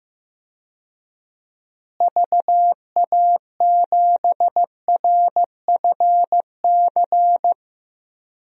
Morse code, heard as a keyed tone, about 15 wpm, 700 Hz. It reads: VA7RFC
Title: DXCC-100 Call Signs - 6 Characters - Repeated 3x - 15wpm